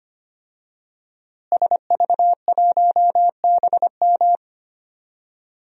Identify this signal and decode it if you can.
Morse code, heard as a keyed tone, about 25 wpm, 700 Hz. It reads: SV1BM
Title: DXCC-100 Call Signs - 5 Characters - Repeated 3x - 25wpm